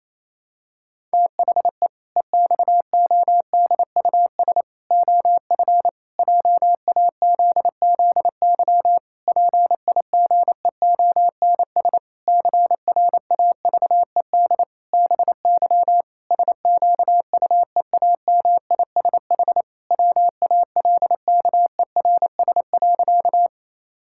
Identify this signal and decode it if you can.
Morse code, heard as a keyed tone, about 28 wpm, 700 Hz. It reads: THE EXODUS OF JAZZY PIGEONS CRAVED BY SQUEAMISH WALKERS.
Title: Pangram Challenge v2 - 28wpm